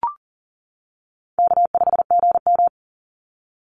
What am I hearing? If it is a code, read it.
Morse code, heard as a keyed tone, about 40 wpm, 700 Hz. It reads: K5GM
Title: Call Signs - 40wpm